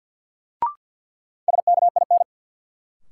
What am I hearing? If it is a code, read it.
Morse code, heard as a keyed tone, about 53 wpm, 700 Hz. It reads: SKIN
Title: Sentences from Top 1300 Words - New Word Drill Alt - 50wpm